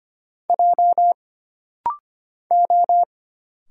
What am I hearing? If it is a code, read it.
Morse code, heard as a keyed tone, about 25 wpm, 700 Hz. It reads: J O
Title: Single Letter-Number Alt - Rapid-Fire 25wpm